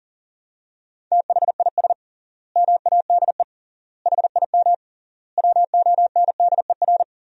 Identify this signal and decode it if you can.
Morse code, heard as a keyed tone, about 40 wpm, 700 Hz. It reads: THIS MADE HIM WONDER
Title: 3x Word-Spacing - Sentences from Top 500 Words - 40wpm